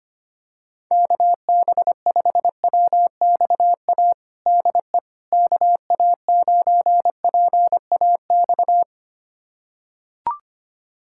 Morse code, heard as a keyed tone, about 25 wpm, 700 Hz. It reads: KB5WXA DE KA9PAX
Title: USA Callsign DE Callsign - 6 Characters - 25wpm